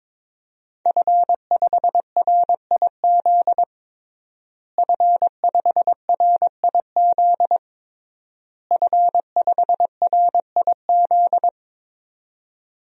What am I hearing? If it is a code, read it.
Morse code, heard as a keyed tone, about 22 wpm, 700 Hz. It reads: F5RIZ F5RIZ F5RIZ
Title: DXCC-100 Call Signs - 5 Characters - Repeated 3x - 22wpm